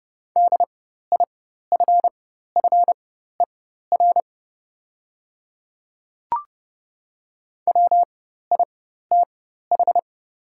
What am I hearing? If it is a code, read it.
Morse code, heard as a keyed tone, about 30 wpm, spaced as for 14 wpm, 700 Hz. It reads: DIFFER WITH
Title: 4x Character Spacing - Top 200 Words - 30wpm